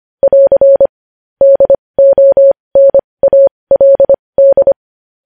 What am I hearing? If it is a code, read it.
Morse code, heard as a keyed tone, about 25 wpm, spaced as for 20 wpm, 550 Hz. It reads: <AR> DONALD